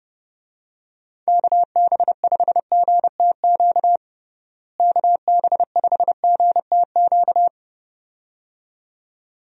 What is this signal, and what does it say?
Morse code, heard as a keyed tone, about 30 wpm, 700 Hz. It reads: KB5GTQ KB5GTQ
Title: USA Call Signs - 6 Characters - Repeated 2x - 30wpm